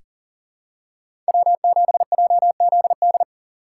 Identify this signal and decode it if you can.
Morse code, heard as a keyed tone, about 40 wpm, 700 Hz. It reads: W7JZD
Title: USA Call Signs - 5 Characters - 3x Character Spacing - 40wpm